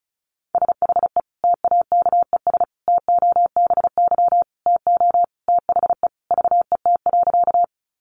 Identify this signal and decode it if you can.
Morse code, heard as a keyed tone, about 34 wpm, 700 Hz. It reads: SHE TAKES TOBY TO THE VET.
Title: Easy Phrases 35wpm